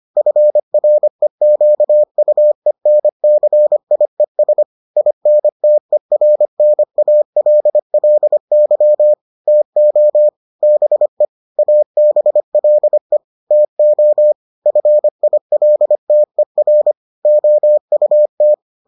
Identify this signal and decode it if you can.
Morse code, heard as a keyed tone, about 25 wpm, 600 Hz. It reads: FREQUENCIES INTERNALLY TO BE ABLE TO FILTER OUT